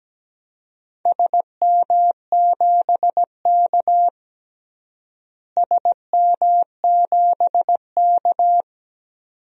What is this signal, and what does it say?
Morse code, heard as a keyed tone, about 17 wpm, 700 Hz. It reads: SM7K SM7K
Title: DXCC-100 Call Signs - 4 Characters - Repeated 2x - 17wpm